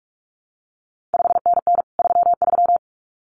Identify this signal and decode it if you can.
Morse code, heard as a keyed tone, about 46 wpm, 700 Hz. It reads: 5NN 33
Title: CQ WW DX Contest Exchange - 45wpm